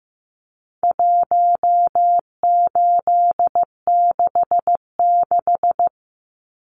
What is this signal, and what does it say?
Morse code, heard as a keyed tone, about 15 wpm, 700 Hz. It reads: 1866
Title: Numbers - 4-Digit 15wpm